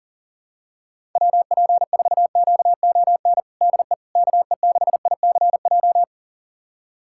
Morse code, heard as a keyed tone, about 40 wpm, 700 Hz. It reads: WP4QON DE KE6ICJ
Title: USA Callsign DE Callsign 40wpm